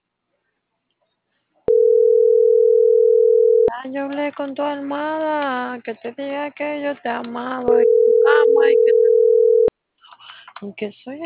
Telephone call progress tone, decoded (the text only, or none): ringback tone